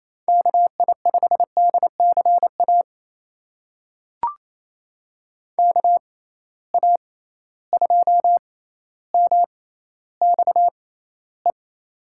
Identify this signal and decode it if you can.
Morse code, heard as a keyed tone, about 28 wpm, spaced as for 14 wpm, 700 Hz. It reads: KI5DCA KA2MXE
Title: USA Call Signs - 6 Characters - 6x Character Spacing - 28wpm